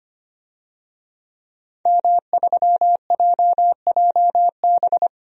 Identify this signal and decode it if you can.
Morse code, heard as a keyed tone, about 25 wpm, 690 Hz. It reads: M3JJB
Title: DXCC-100 Call Signs - 5 Characters - 25wpm